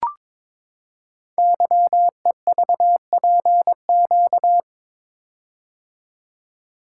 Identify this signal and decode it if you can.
Morse code, heard as a keyed tone, about 22 wpm, 700 Hz. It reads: YEVPQ
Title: Code Groups - 5 Letters - 22wpm